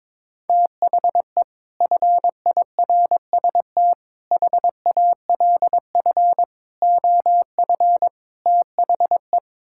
Morse code, heard as a keyed tone, about 22 wpm, 700 Hz. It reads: THE FIRST HALF OF THE
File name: practice / 22wpm / Build A Phrase - 5-Word Phrases - 22wpm